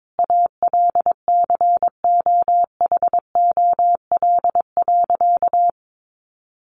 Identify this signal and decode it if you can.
Morse code, heard as a keyed tone, about 22 wpm, 700 Hz. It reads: ALCOHOL.
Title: Sentences from Top 1400 Words 22wpm